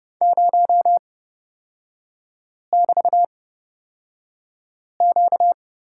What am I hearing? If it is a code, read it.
Morse code, heard as a keyed tone, about 30 wpm, spaced as for 5 wpm, 690 Hz. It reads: 0<BT>Q
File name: course / 30:10wpm / 236 Lesson 38e - Letter q - Three Characters 30wpm